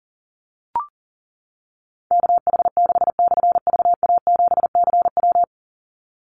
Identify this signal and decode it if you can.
Morse code, heard as a keyed tone, about 40 wpm, 700 Hz. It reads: KH6/VA7CW
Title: Call Signs - 40wpm